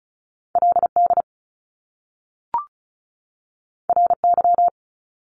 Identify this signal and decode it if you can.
Morse code, heard as a keyed tone, about 36 wpm, 700 Hz. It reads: LD RY
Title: Top Two Letter Combinations in Words 35wpm